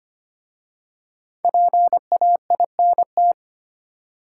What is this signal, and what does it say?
Morse code, heard as a keyed tone, about 25 wpm, 700 Hz. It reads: PAINT
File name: practice / 25wpm / Easy Words with Hints - 25wpm